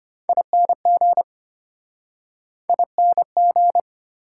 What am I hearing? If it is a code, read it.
Morse code, heard as a keyed tone, about 33 wpm, 700 Hz. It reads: ING ING